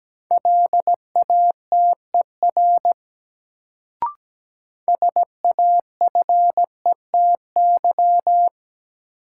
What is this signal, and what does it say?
Morse code, heard as a keyed tone, about 17 wpm, 700 Hz. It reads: LATER SAFETY
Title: Sentences from Top 700 Words - New Word Drill - 17wpm